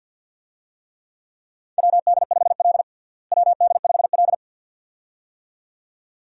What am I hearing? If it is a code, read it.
Morse code, heard as a keyed tone, about 52 wpm, 700 Hz. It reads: WD5L WD5L
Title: USA Call Signs - 4 Characters - Repeated 2x - 50wpm